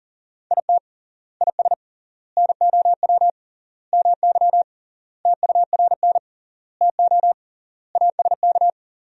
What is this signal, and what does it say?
Morse code, heard as a keyed tone, about 37 wpm, 700 Hz. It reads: IT IS NOW MY TURN TO ASK
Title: 3x Word-Spacing - Sentences from Top 200 Words - 40wpm